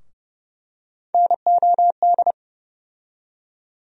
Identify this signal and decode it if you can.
Morse code, heard as a keyed tone, about 32 wpm, 700 Hz. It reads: NOD
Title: USA Call Sign Postfixes - 3 Letters 30wpm